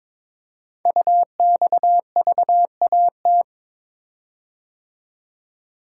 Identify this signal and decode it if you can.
Morse code, heard as a keyed tone, about 22 wpm, 700 Hz. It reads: UXVAT
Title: Code Groups - 5 Letters - 22wpm